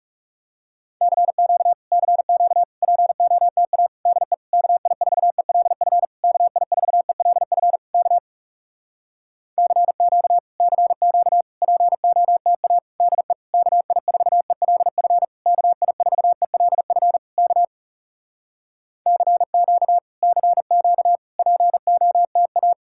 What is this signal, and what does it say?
Morse code, heard as a keyed tone, about 40 wpm, 700 Hz. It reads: CQ CQ POTA DE KI4ELF KI4ELF K CQ CQ POTA DE KI4ELF KI4ELF K CQ CQ POTA